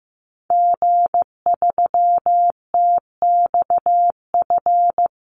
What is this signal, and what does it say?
Morse code, heard as a keyed tone, about 15 wpm, 700 Hz. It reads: G3TXF
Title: Call Signs Repeated 3x - 15wpm